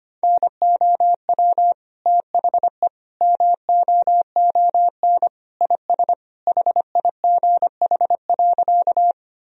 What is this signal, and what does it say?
Morse code, heard as a keyed tone, about 25 wpm, 700 Hz. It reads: NOW THE MOON IS HIGH.
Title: Sentences from Top 1000 Words - Review Part 3 - 25wpm